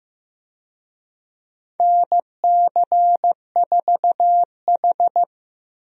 Morse code, heard as a keyed tone, about 15 wpm, 700 Hz. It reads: NC4H
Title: USA Call Signs - 4 Characters - Repeated 3x - 15wpm